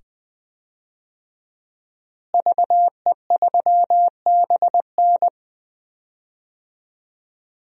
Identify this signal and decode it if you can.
Morse code, heard as a keyed tone, about 20 wpm, 700 Hz. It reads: VE3BN
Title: DXCC-100 Call Signs - 5 Characters - Repeated 2x - 20wpm